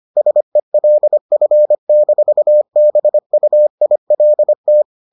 Morse code, heard as a keyed tone, about 25 wpm, 600 Hz. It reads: SELF-BUILT